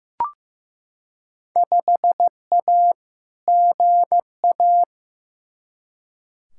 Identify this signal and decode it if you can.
Morse code, heard as a keyed tone, about 15 wpm, 700 Hz. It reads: 5A GA
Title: Field Day Exchanges - 15wpm